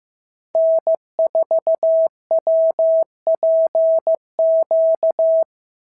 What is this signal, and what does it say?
Morse code, heard as a keyed tone, about 15 wpm, 650 Hz. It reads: N4WPQ
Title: SOTA QSO Practice 15wpm